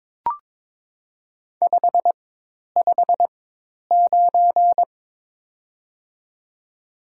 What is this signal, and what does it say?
Morse code, heard as a keyed tone, about 23 wpm, spaced as for 10 wpm, 700 Hz. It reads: 559
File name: Top 90 QSO Elements - 4x Character Spacing - 22wpm